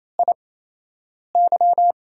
Morse code, heard as a keyed tone, about 28 wpm, spaced as for 8 wpm, 700 Hz. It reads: IY
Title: USA Call Signs - 8x Character Spacing - 28wpm